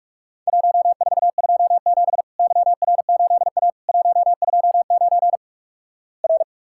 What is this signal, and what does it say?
Morse code, heard as a keyed tone, about 46 wpm, 700 Hz. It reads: 1427 YR8A 129 R
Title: Worked All Europe DX Contest - QTC Exchange - 45wpm